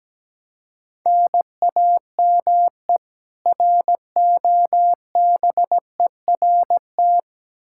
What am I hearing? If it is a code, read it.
Morse code, heard as a keyed tone, about 17 wpm, 700 Hz. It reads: NAME ROBERT
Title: Practice QSO - Exchange 1 - Beginning - 17wpm